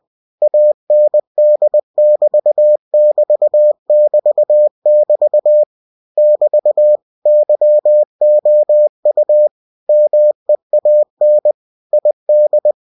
Morse code, heard as a keyed tone, about 20 wpm, 600 Hz. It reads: AND<BT><BT><BT><BT> <BT> YOU MEAN ID
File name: cw-books / winnie-the-pooh / winnie-20wpm-page-0018